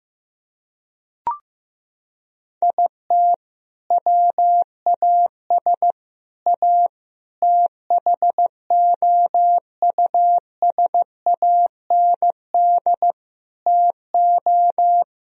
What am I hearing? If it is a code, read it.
Morse code, heard as a keyed tone, about 15 wpm, 700 Hz. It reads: IT WAS A THOUSAND TO
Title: Sentences from Top 1000 Words - Review Part 3 - 15wpm